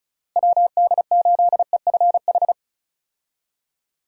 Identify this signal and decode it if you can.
Morse code, heard as a keyed tone, about 34 wpm, 700 Hz. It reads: WD8EFH
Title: USA Call Signs - 6 Characters - Repeated 3x - 35wpm